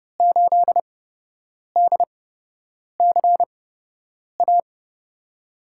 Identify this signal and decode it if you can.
Morse code, heard as a keyed tone, about 30 wpm, spaced as for 8 wpm, 700 Hz. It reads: 8DCA
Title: USA Call Signs - 6 Characters - 8x Character Spacing - 30wpm